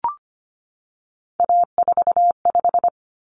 Morse code, heard as a keyed tone, about 25 wpm, 700 Hz. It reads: A45
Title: Common Cut Numbers - 3-Digit 25wpm